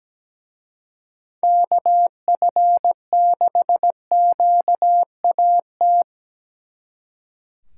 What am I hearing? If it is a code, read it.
Morse code, heard as a keyed tone, about 20 wpm, 700 Hz. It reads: KF6QAT
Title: Speed Racing - USA 2x3 Call Signs 17wpm